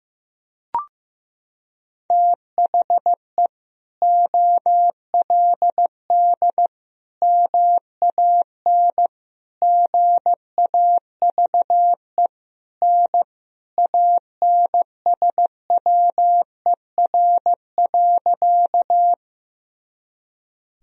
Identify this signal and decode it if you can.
Morse code, heard as a keyed tone, about 16 wpm, 700 Hz. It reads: THE OLD MAN GAVE N ANSWER.
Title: Sentences from Top 500 Words and Abbreviations 15wpm